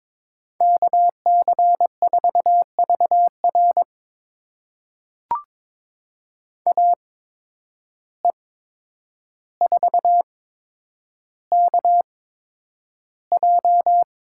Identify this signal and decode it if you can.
Morse code, heard as a keyed tone, about 21 wpm, spaced as for 9 wpm, 700 Hz. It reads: KC4VR AE4KJ
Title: USA Call Signs - 5 Characters - 8x Character Spacing - 22wpm